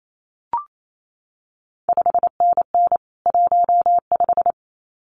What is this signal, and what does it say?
Morse code, heard as a keyed tone, about 28 wpm, 700 Hz. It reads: 5NN 15
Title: CQ WW DX Contest Exchange - 28wpm